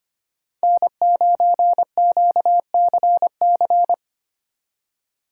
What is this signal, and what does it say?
Morse code, heard as a keyed tone, about 32 wpm, 700 Hz. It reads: N9QCC